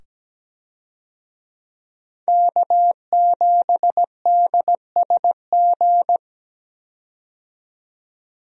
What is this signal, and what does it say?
Morse code, heard as a keyed tone, about 17 wpm, 700 Hz. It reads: K7DSG
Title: USA Call Signs - 5 Characters - 17wpm